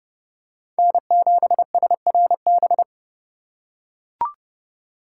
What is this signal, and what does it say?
Morse code, heard as a keyed tone, about 27 wpm, 700 Hz. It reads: N7SRB